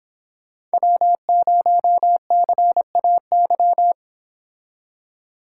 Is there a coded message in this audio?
Yes: Morse code from 0.7 to 3.9 s, heard as a keyed tone, about 26 wpm, 700 Hz. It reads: W0CAY